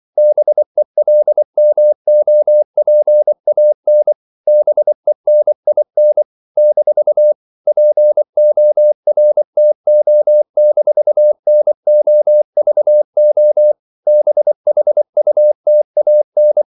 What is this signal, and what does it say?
Morse code, heard as a keyed tone, about 24 wpm, 600 Hz. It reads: BELMOPAN BENIN - PORTO-NOVO BHUTAN
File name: CW / Practice / NationalCapitals24